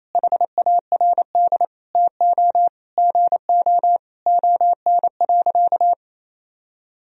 Morse code, heard as a keyed tone, about 28 wpm, 700 Hz. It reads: HARD TO GO ON.
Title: Sentences from Top 1000 Words - Review Part 1 - 28wpm